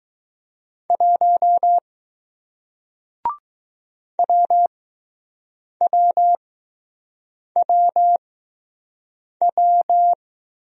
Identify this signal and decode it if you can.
Morse code, heard as a keyed tone, about 20 wpm, spaced as for 6 wpm, 700 Hz. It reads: 1 WWWW